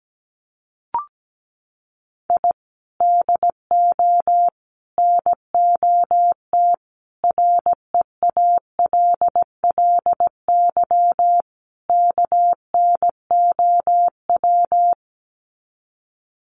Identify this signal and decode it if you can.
Morse code, heard as a keyed tone, about 17 wpm, 700 Hz. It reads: I DO NOT REALLY KNOW
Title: Sets of 5 Words Encore 17wpm